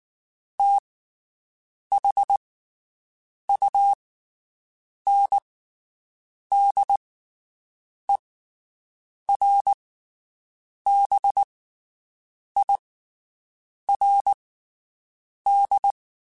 Morse code, heard as a keyed tone, about 19 wpm, spaced as for 7 wpm, 800 Hz. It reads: THUNDERBIRD
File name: (spaced) / QSOs Spaced-mp3s / WD0BGP 1 @18 sp